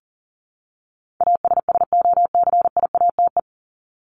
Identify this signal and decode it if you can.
Morse code, heard as a keyed tone, about 40 wpm, 700 Hz. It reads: ASSOCIATE
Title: ED Suffix - Top 100 Words with Base Words 40wpm